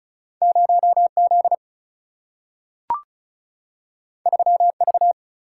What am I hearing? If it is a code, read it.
Morse code, heard as a keyed tone, about 36 wpm, 700 Hz. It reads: 0 Z 3 V
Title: Single-Letter Single-Number 35wpm